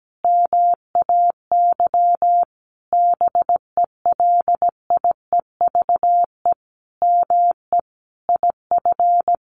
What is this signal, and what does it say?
Morse code, heard as a keyed tone, about 17 wpm, 700 Hz. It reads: MAY BELIEVE ME IF
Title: Sentences from Top 1000 Words 17wpm